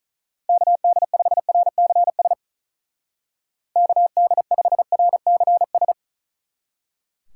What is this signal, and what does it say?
Morse code, heard as a keyed tone, about 44 wpm, 700 Hz. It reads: KD5RCS KD5RCS